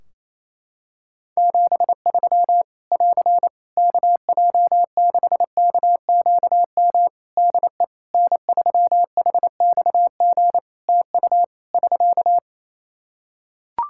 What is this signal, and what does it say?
Morse code, heard as a keyed tone, about 28 wpm, 700 Hz. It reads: 73 <AR> KJ6KQM DE N3HXG TU <SK>
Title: Practice QSO - Exchange 4 - Ending - 28wpm